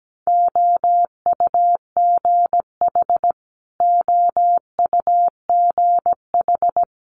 Morse code, heard as a keyed tone, about 17 wpm, 700 Hz. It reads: OUGH OUGH
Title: Multi-letter Phonemes Repeated 3x 17wpm